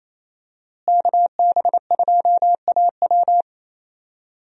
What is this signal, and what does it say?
Morse code, heard as a keyed tone, about 28 wpm, 700 Hz. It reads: KB2AW
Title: USA Call Signs - 5 Characters - Repeated 3x - 28wpm